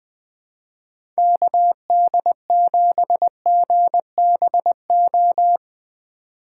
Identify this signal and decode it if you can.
Morse code, heard as a keyed tone, about 20 wpm, 700 Hz. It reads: KD7GBO
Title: Speed Racing - USA 2x3 Call Signs 17wpm